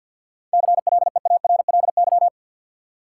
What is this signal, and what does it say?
Morse code, heard as a keyed tone, about 52 wpm, 700 Hz. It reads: CLEARLY
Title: Sentences from Top 1300 Words - New Word Drill Alt - 50wpm